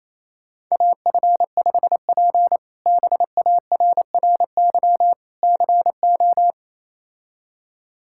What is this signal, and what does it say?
Morse code, heard as a keyed tone, about 27 wpm, 700 Hz. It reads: AF5P BARRY CO
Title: K1USN SST QSO Practice - 28wpm